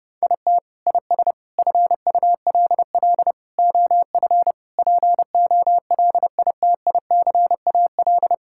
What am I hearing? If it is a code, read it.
Morse code, heard as a keyed tone, about 30 wpm, 700 Hz. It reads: IT IS FULL OF POLITICAL
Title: Sentences from Top 1000 Words - Review Part 5 - 30wpm